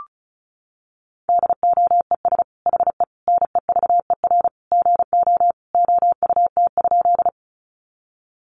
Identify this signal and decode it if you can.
Morse code, heard as a keyed tone, about 34 wpm, 700 Hz. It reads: DOES HE NEVER GO OUT?
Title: Sentences from Top 300 Words 35wpm